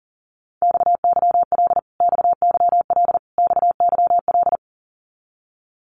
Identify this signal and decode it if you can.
Morse code, heard as a keyed tone, about 40 wpm, 700 Hz. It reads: XYL XYL XYL
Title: Abbreviations Repeated 3x - 40wpm